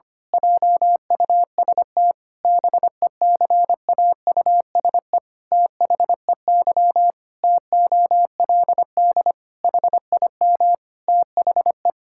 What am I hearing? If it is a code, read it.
Morse code, heard as a keyed tone, about 25 wpm, 700 Hz. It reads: JUST BECAUSE THEY TOLD HIM THE
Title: Pangram Challenge v2 - 25wpm